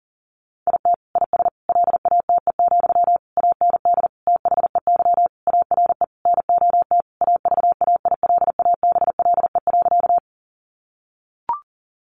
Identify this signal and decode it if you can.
Morse code, heard as a keyed tone, about 40 wpm, 700 Hz. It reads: IT IS LATE, AND THEY ARE NOT AVAILABLE.
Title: Sentences from Top 600 Words 40wpm